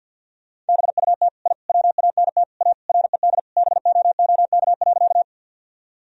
Morse code, heard as a keyed tone, about 51 wpm, 700 Hz. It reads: BUT I WANT A RED BOOK.
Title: Sentences from Top 1000 Words - Review Part 2 - 50wpm